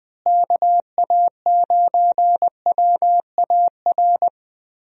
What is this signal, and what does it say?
Morse code, heard as a keyed tone, about 20 wpm, 700 Hz. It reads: KA9WAR
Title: USA Call Signs - 6 Characters - Repeated 2x - 20wpm